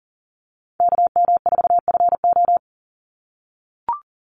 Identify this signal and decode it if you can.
Morse code, heard as a keyed tone, about 40 wpm, 700 Hz. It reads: KM4FO
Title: Call Signs - 40wpm